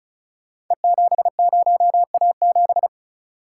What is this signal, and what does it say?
Morse code, heard as a keyed tone, about 36 wpm, 700 Hz. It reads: E70A7
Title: Code Groups - 5 Letter:Numbers - 35wpm